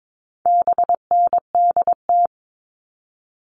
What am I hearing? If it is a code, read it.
Morse code, heard as a keyed tone, about 22 wpm, 700 Hz. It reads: BNDT
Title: All Cut Numbers - 4-Digit 22wpm